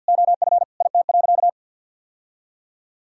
Morse code, heard as a keyed tone, about 46 wpm, 700 Hz. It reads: OF IT.
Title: Sentences from Top 1000 Words - Review Part 6 - 50wpm